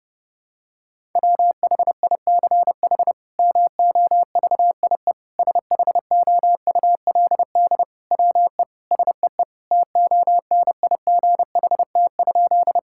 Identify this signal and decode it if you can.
Morse code, heard as a keyed tone, about 30 wpm, 700 Hz. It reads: WHICH MOVIE SHOULD WE SEE TONIGHT?
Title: Sentences from Top 1000 Words - Review Part 5 - 30wpm